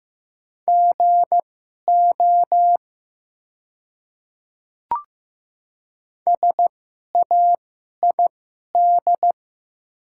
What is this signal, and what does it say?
Morse code, heard as a keyed tone, about 15 wpm, spaced as for 11 wpm, 700 Hz. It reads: GO SAID
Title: 2x Character Spacing - Top 500 Words - 15wpm